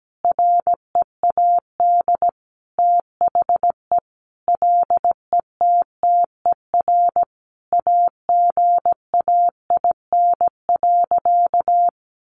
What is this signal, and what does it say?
Morse code, heard as a keyed tone, about 17 wpm, 700 Hz. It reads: READ THE LETTER AGAIN.
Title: Sentences from Top 300 Words 17wpm